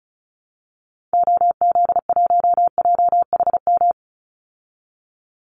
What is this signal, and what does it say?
Morse code, heard as a keyed tone, about 34 wpm, 700 Hz. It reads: OZ1JHM
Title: Call Signs Repeated 2x - 35wpm